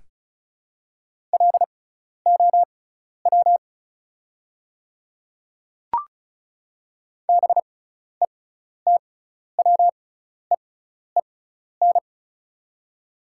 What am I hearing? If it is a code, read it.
Morse code, heard as a keyed tone, about 34 wpm, spaced as for 12 wpm, 700 Hz. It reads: LOW BETWEEN
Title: 6x Character Spacing - Top 300 Words - 35wpm